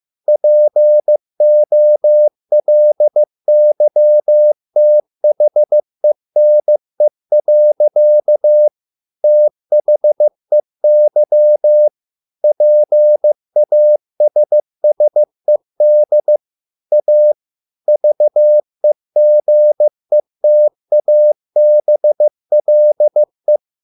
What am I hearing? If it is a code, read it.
Morse code, heard as a keyed tone, about 15 wpm, 600 Hz. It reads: POLYTHENE. THEY PASSED A VEGETABLE